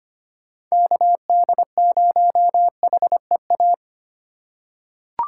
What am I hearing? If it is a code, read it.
Morse code, heard as a keyed tone, about 25 wpm, 700 Hz. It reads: KD0HEA
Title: USA Call Signs - 6 Characters - 4x Character Spacing - 25wpm